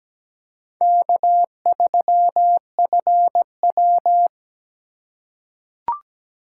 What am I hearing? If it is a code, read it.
Morse code, heard as a keyed tone, about 17 wpm, 700 Hz. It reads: K3FW
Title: USA Call Signs - 4 Characters - 3x Character Spacing - 17wpm